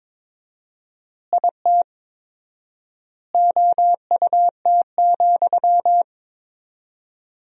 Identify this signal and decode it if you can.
Morse code, heard as a keyed tone, about 21 wpm, 700 Hz. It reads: IT OUT,
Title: 4x Word-Spacing - Sentences from Top 300 Words - 22wpm